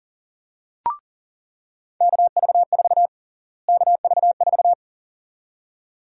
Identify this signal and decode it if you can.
Morse code, heard as a keyed tone, about 38 wpm, 700 Hz. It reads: KV4 KV4
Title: USA Call Sign Prefixes - Repeated 2x - 40wpm